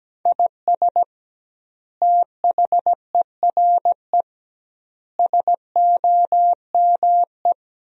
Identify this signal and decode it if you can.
Morse code, heard as a keyed tone, about 17 wpm, 700 Hz. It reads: IS THERE SOME
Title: 2x Word-Spacing - Sentences from Top 200 Words - 17wpm